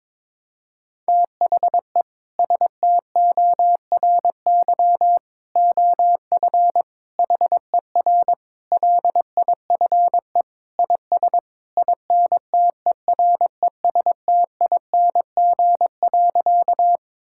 Morse code, heard as a keyed tone, about 22 wpm, 700 Hz. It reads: THE STORY OF HER LIFE IS INTERESTING.
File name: practice / 22wpm / Sentences with 3-Word Phrases - 22wpm